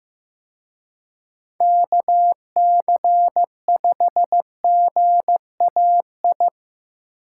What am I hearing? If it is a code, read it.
Morse code, heard as a keyed tone, about 18 wpm, 700 Hz. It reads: KC5GAI
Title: Speed Racing - USA 2x3 Call Signs 15wpm